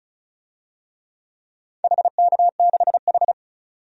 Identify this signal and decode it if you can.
Morse code, heard as a keyed tone, about 35 wpm, 700 Hz. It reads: HK6H
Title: DXCC-100 Call Signs - 4 Characters - 35wpm